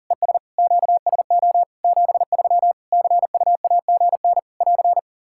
Morse code, heard as a keyed tone, about 40 wpm, 700 Hz. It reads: ES QSO 73 CUAGN <AR>
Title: K1USN SST QSO Practice 40wpm